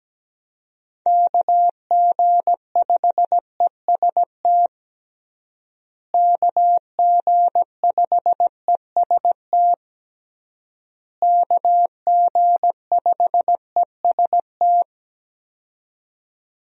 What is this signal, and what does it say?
Morse code, heard as a keyed tone, about 17 wpm, 700 Hz. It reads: KG5EST KG5EST KG5EST
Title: USA Call Signs - 6 Characters - Repeated 3x - 17wpm